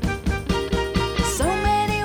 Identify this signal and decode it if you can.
There is no code.